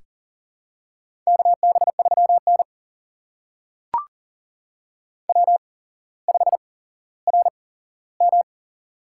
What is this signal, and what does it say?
Morse code, heard as a keyed tone, about 41 wpm, spaced as for 17 wpm, 700 Hz. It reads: KB3N W5RM
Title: USA Call Signs - 4 Characters - 8x Character Spacing - 40wpm